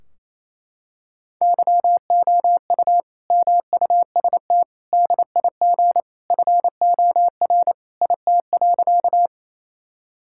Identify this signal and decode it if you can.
Morse code, heard as a keyed tone, about 28 wpm, 700 Hz. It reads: YOU MUST DIG FOR IT.
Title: Sentences from Top 1600 Words 28wpm